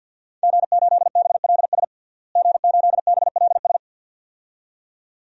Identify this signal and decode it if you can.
Morse code, heard as a keyed tone, about 46 wpm, 700 Hz. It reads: G8BLS G8BLS
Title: DXCC-100 Call Signs - 5 Characters - Repeated 2x - 50wpm